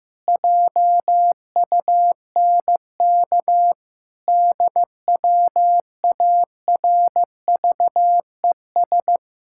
Morse code, heard as a keyed tone, about 16 wpm, 700 Hz. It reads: JUNK DWARVES
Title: Pangram Challenge v2 - 15wpm